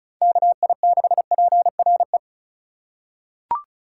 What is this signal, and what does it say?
Morse code, heard as a keyed tone, about 34 wpm, 700 Hz. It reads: KI6PRE
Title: USA Call Signs - 6 Characters - 4x Character Spacing - 35wpm